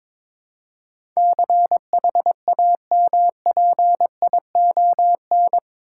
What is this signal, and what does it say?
Morse code, heard as a keyed tone, about 22 wpm, 700 Hz. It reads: CHAMPION
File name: practice / 22wpm / Eight Letter Words - 22wpm